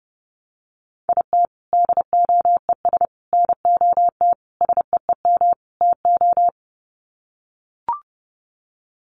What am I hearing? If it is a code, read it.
Morse code, heard as a keyed tone, about 30 wpm, 700 Hz. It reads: IT DOES NOT SEEM TO
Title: Sets of 5 Words Encore2 30wpm